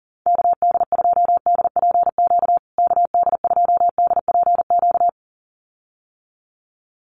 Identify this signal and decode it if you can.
Morse code, heard as a keyed tone, about 40 wpm, 700 Hz. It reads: KD2DPQ KD2DPQ
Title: USA Call Signs Repeated 2x 40wpm